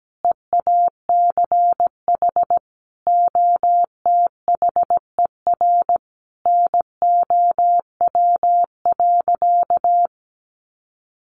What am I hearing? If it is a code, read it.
Morse code, heard as a keyed tone, about 17 wpm, 700 Hz. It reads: EACH OTHER NOW.
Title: Sentences from Top 1200 Words 17wpm